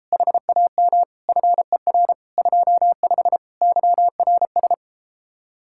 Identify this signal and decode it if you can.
Morse code, heard as a keyed tone, about 33 wpm, 700 Hz. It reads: HAM FER 25 YRS